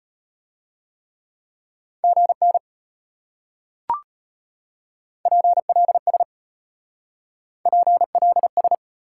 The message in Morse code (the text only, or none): GN PLS PLS